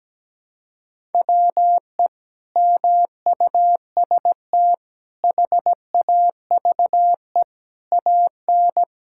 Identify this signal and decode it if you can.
Morse code, heard as a keyed tone, about 17 wpm, 700 Hz. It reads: WE MUST HAVE AN
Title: Sentences from Top 1000 Words - Review Part 6 - 17wpm